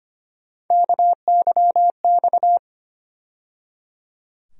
Morse code, heard as a keyed tone, about 25 wpm, 700 Hz. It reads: KYX
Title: USA Call Sign Postfixes - 3 Letters 25wpm